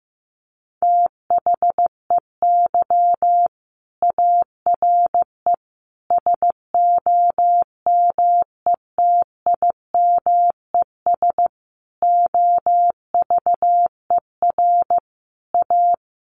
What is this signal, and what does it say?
Morse code, heard as a keyed tone, about 15 wpm, 700 Hz. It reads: THEY ARE SOMETIMES OVER A
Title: Sentences from Top 1300 Words 15wpm